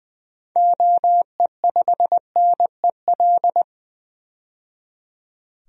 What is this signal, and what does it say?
Morse code, heard as a keyed tone, about 20 wpm, 700 Hz. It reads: OE5NEL
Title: DXCC-100 Call Signs - 6 Characters - 20wpm